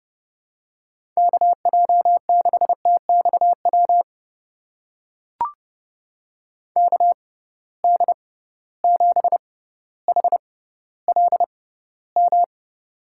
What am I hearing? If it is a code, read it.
Morse code, heard as a keyed tone, about 31 wpm, spaced as for 16 wpm, 700 Hz. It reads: KJ6TXW KD7HLM
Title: USA Call Signs - 6 Characters - 6x Character Spacing - 30wpm